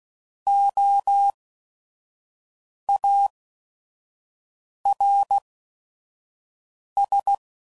Morse code, heard as a keyed tone, about 16 wpm, spaced as for 5 wpm, 800 Hz. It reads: OARS
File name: Prefixes / By First Letter / Os